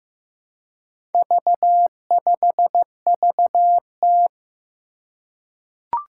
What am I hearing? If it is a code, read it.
Morse code, heard as a keyed tone, about 15 wpm, 700 Hz. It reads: V5VT
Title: Four Challenging Letter-Number Combinations 15wpm